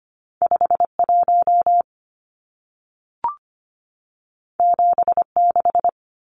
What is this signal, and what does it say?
Morse code, heard as a keyed tone, about 25 wpm, 700 Hz. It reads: 51 76
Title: Numbers - 2-Digit 25wpm